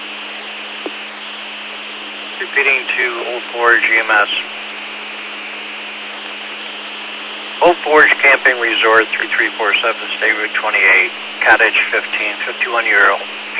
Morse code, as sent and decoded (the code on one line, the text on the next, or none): none